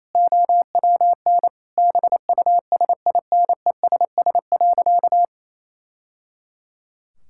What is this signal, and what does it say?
Morse code, heard as a keyed tone, about 28 wpm, 700 Hz. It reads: OWN BUSINESS.
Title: Sentences from Top 1000 Words - Review Part 6 - 28wpm